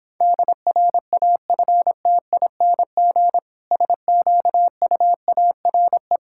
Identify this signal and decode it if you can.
Morse code, heard as a keyed tone, about 26 wpm, 700 Hz. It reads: DRAFTING SQUARE